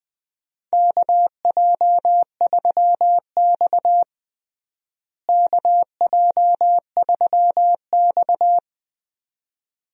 Morse code, heard as a keyed tone, about 20 wpm, 700 Hz. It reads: KJ3X KJ3X
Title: USA Call Signs - 4 Characters - Repeated 2x - 20wpm